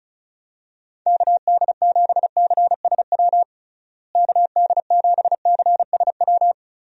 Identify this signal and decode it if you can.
Morse code, heard as a keyed tone, about 34 wpm, 700 Hz. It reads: KD7CSW KD7CSW
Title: USA Call Signs - 6 Characters - Repeated 3x - 35wpm